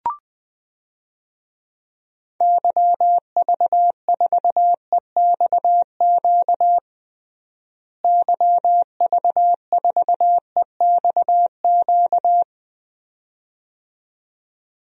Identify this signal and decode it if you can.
Morse code, heard as a keyed tone, about 20 wpm, 700 Hz. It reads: YV4EXQ YV4EXQ
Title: DXCC-100 Call Signs - 6 Characters - Repeated 2x - 20wpm